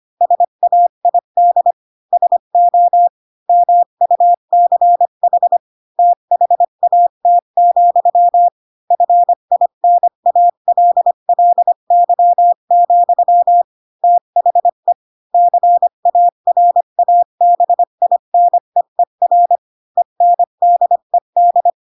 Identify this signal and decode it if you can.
Morse code, heard as a keyed tone, about 25 wpm, 700 Hz. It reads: SAID SO MUCH THAT, FINALLY, THE CARABINEER ENDED
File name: Pinocchio / Pinocchio ch3 20 25 ik1hge